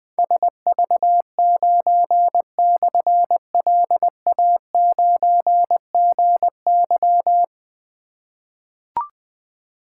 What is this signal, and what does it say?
Morse code, heard as a keyed tone, about 20 wpm, 700 Hz. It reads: SV9/LA9GY
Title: Difficult Call Signs 20wpm